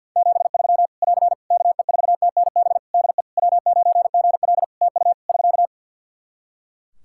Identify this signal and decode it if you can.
Morse code, heard as a keyed tone, about 50 wpm, 700 Hz. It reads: 73 <AR> KE4TNB DE W9GL TU <SK>
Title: K1USN SST QSO Practice 50wpm